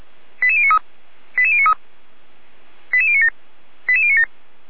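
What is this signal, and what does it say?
Five-tone selective call (ZVEI), heard as quick runs of five tones, tones of about 70 ms. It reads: 88093 88093 88097 88097
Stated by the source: ZVEI five-tone selective call 88055, received with an SDR.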